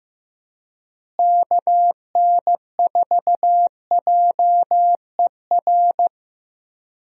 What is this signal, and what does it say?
Morse code, heard as a keyed tone, about 15 wpm, 700 Hz. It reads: KN4JER
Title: USA Call Signs - 6 Characters - Repeated 2x - 15wpm